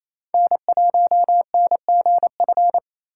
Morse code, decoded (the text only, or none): N1NGF